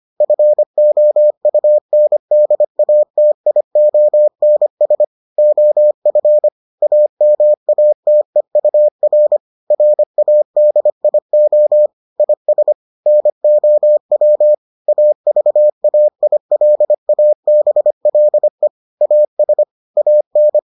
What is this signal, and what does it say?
Morse code, heard as a keyed tone, about 25 wpm, 600 Hz. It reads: FOUNDATIONS OF AMATEUR RADIO IS NOW AVAILABLE AS AN